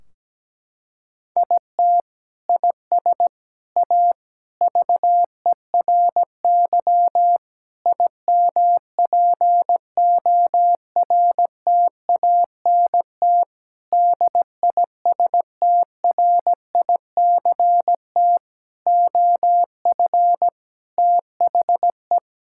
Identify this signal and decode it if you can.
Morse code, heard as a keyed tone, about 17 wpm, 700 Hz. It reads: IT IS A VERY IMPORTANT DISTRICT OF THE
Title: Sentences from Top 1600 Words 17wpm